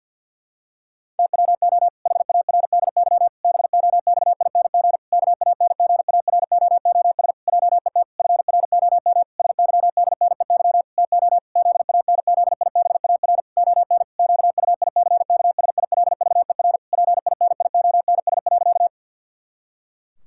Morse code, heard as a keyed tone, about 51 wpm, 700 Hz. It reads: TWO HARDY BOXING KANGAROOS JET FROM SYDNEY TO ZANZIBAR ON QUICKSILVER PINIONS.
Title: Pangram Challenge v2 - 50wpm